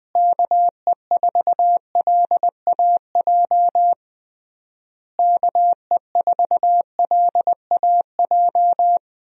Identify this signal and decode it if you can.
Morse code, heard as a keyed tone, about 21 wpm, 700 Hz. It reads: KE4LAJ KE4LAJ
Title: USA Call Signs - 6 Characters - Repeated 3x - 20wpm